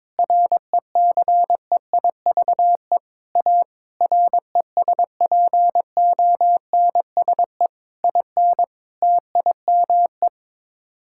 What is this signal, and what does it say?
Morse code, heard as a keyed tone, about 22 wpm, 700 Hz. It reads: RECEIVE A RESPONSE IN TIME
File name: practice / 22wpm / Sets of 8 Words - 22wpm